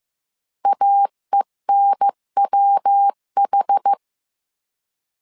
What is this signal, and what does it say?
Morse code, heard as a keyed tone, about 15 wpm, 800 Hz. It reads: AENWH
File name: CW / DET / 1 3 1500C S A 14wpm 2447